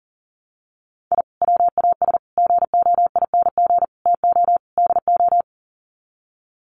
Morse code, heard as a keyed tone, about 40 wpm, 700 Hz. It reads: I WAS GOING TO DO
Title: Sets of 5 Words Encore2 40wpm